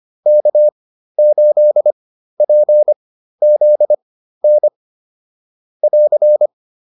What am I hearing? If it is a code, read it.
Morse code, heard as a keyed tone, about 25 wpm, spaced as for 13 wpm, 600 Hz. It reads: K8PZN <AR>